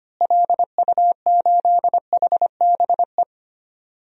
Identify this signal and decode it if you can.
Morse code, heard as a keyed tone, about 25 wpm, 700 Hz. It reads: LU8HBE
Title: DXCC-100 Call Signs - 6 Characters - Repeated 3x - 25wpm